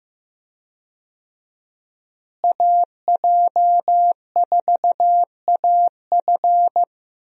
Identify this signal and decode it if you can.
Morse code, heard as a keyed tone, about 16 wpm, 700 Hz. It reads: AJ4AF
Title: USA Call Signs - 5 Characters - Repeated 3x - 15wpm